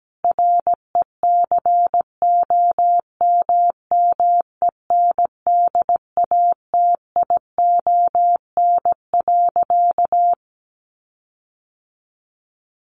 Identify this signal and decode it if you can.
Morse code, heard as a keyed tone, about 17 wpm, 700 Hz. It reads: RECOMMENDATION.
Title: Sentences from Top 1400 Words 17wpm